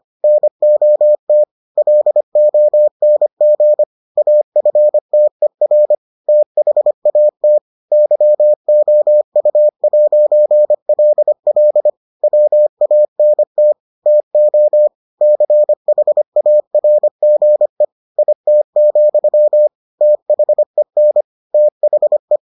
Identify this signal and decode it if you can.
Morse code, heard as a keyed tone, about 25 wpm, 600 Hz. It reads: NOT LONG AFTER THAT YOU'LL WANT TO CHARGE IT, THEN THE